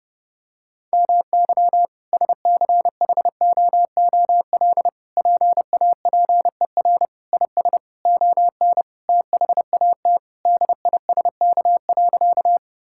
Morse code, heard as a keyed tone, about 32 wpm, 700 Hz. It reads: MY SCHOOL PAPER IS ON THAT DISK.
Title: Sentences from Top 1000 Words - Review Part 6 - 30wpm